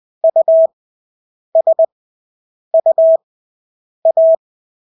Morse code, heard as a keyed tone, about 20 wpm, spaced as for 8 wpm, 650 Hz. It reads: USUA
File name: Engelske ord 20/8 wpm